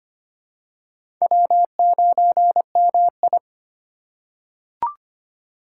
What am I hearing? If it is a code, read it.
Morse code, heard as a keyed tone, about 25 wpm, 700 Hz. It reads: W9MI
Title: USA Call Signs - 4 Characters - 4x Character Spacing - 25wpm